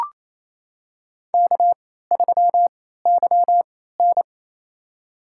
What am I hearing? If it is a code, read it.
Morse code, heard as a keyed tone, about 28 wpm, spaced as for 16 wpm, 700 Hz. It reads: K3YN
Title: USA Call Signs - 3x Character Spacing - 28wpm